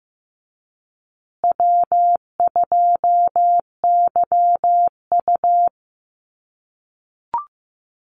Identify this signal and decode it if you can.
Morse code, heard as a keyed tone, about 15 wpm, 700 Hz. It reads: W2YU
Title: USA Call Signs Repeated 3x 15wpm